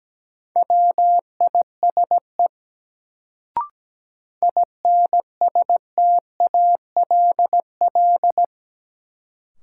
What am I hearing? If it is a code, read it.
Morse code, heard as a keyed tone, about 17 wpm, 700 Hz. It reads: WISE INSTALL
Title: Sentences from Top 1600 Words - New Word Drill Alt - 17wpm